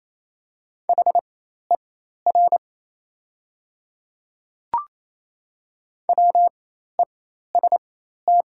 Morse code, heard as a keyed tone, about 27 wpm, spaced as for 13 wpm, 690 Hz. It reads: HER WEST
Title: 4x Character Spacing - Top 500 Words - 28wpm